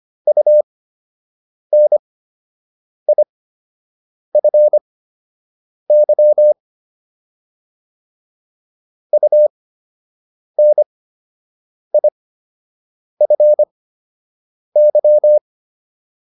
Morse code, heard as a keyed tone, about 25 wpm, spaced as for 7 wpm, 600 Hz. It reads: UNIFY UNIFY